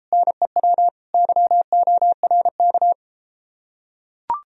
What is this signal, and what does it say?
Morse code, heard as a keyed tone, about 33 wpm, 700 Hz. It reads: NEW YORK